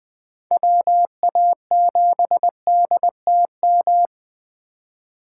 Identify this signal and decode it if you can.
Morse code, heard as a keyed tone, about 20 wpm, 700 Hz. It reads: WA7DTM
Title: USA Call Signs - 6 Characters - 2x Character Spacing - 20wpm